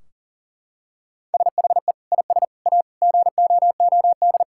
Morse code, heard as a keyed tone, about 37 wpm, 700 Hz. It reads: SHE IS A GOOD
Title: Sentences from Top 1600 Words 40wpm